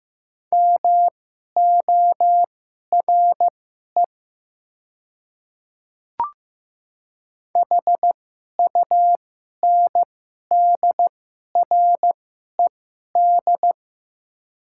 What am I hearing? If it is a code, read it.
Morse code, heard as a keyed tone, about 15 wpm, spaced as for 11 wpm, 700 Hz. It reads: MORE HUNDRED
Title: 2x Character Spacing - Top 500 Words - 15wpm